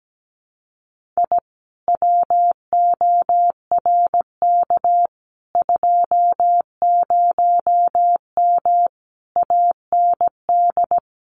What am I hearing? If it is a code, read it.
Morse code, heard as a keyed tone, about 17 wpm, 700 Hz. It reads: I WORK 20M AND
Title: Common QSO Phrases Encore 17wpm